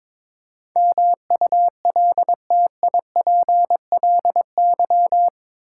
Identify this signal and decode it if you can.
Morse code, heard as a keyed tone, about 21 wpm, 700 Hz. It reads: MULTIPLY
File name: practice / 22wpm / Eight Letter Words - 22wpm